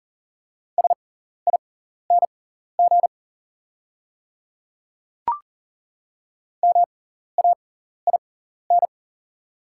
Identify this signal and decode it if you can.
Morse code, heard as a keyed tone, about 40 wpm, spaced as for 14 wpm, 690 Hz. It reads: SING MAIN
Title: 6x Character Spacing - Top 500 Words - 40wpm